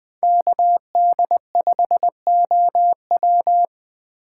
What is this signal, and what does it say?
Morse code, heard as a keyed tone, about 20 wpm, 700 Hz. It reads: KD5OW
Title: Speed Racing - USA 2x2 Call Signs 17wpm